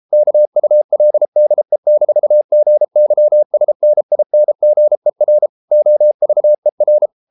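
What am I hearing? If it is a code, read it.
Morse code, heard as a keyed tone, about 33 wpm, 600 Hz. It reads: KULDE-GYSNINGER OVER
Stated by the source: Morse code at 33 wpm, QRQ.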